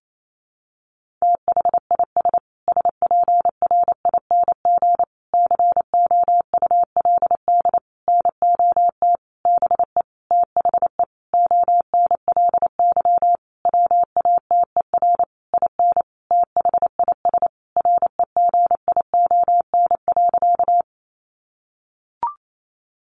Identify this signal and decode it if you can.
Morse code, heard as a keyed tone, about 28 wpm, 700 Hz. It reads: THIS SPRING COULD NOT BE THE ONLY WATER IN THIS REGION.
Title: Sentences from Top 1500 Words 28wpm